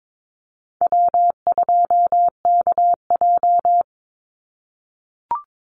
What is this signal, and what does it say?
Morse code, heard as a keyed tone, about 22 wpm, 700 Hz. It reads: W2KJ
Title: Call Signs - 22wpm